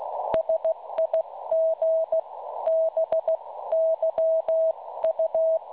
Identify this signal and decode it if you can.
Morse code, heard as a keyed tone, about 16 wpm, 650 Hz. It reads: SIG BYU
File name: cwronde / websdr recording .6kHz